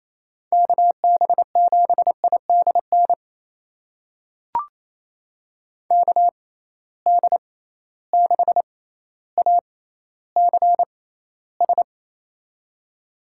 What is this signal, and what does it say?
Morse code, heard as a keyed tone, about 28 wpm, spaced as for 14 wpm, 700 Hz. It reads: KB7IDN KD6ACS
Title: USA Call Signs - 6x Character Spacing - 28wpm